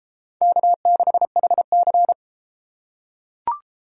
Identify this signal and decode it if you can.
Morse code, heard as a keyed tone, about 31 wpm, 700 Hz. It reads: K6HC